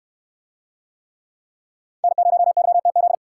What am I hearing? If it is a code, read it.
Morse code, heard as a keyed tone, about 63 wpm, 700 Hz. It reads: N0YTZ